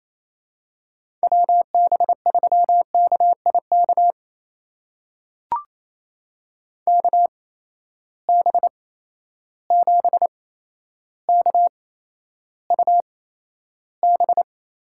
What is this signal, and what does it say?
Morse code, heard as a keyed tone, about 28 wpm, spaced as for 12 wpm, 700 Hz. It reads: WB3KIK KB7KUB
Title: USA Call Signs - 6 Characters - 8x Character Spacing - 28wpm